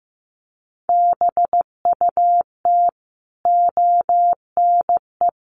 Morse code, heard as a keyed tone, about 15 wpm, 700 Hz. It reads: BUT ONE
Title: Sentences from Top 1400 Words 15wpm